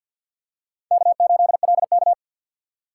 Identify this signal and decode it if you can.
Morse code, heard as a keyed tone, about 48 wpm, 700 Hz. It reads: K8LK
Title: USA Call Signs - 2x Character Spacing - 50wpm